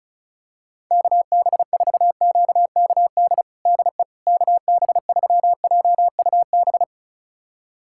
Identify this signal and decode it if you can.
Morse code, heard as a keyed tone, about 36 wpm, 700 Hz. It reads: KB4QKD DE KB3JUB
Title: USA Callsign DE Callsign - 6 Characters - 35wpm